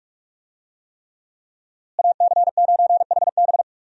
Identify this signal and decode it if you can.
Morse code, heard as a keyed tone, about 46 wpm, 700 Hz. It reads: AC9HB
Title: USA Call Signs - 5 Characters - Repeated 3x - 45wpm